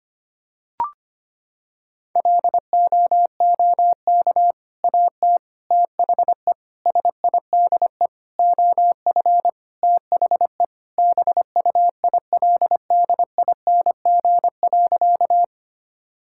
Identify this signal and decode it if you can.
Morse code, heard as a keyed tone, about 25 wpm, 700 Hz. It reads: LOOK AT THE SIDE OF THE BUILDING.
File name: practice / 25wpm / Sentences with 3-Word Phrases - 25wpm